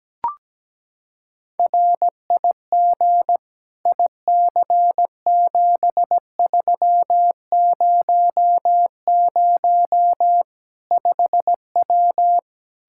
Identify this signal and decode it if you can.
Morse code, heard as a keyed tone, about 17 wpm, 700 Hz. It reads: RIG IC7300 5W
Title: Practice QSO - Exchange 2 - Info - 17wpm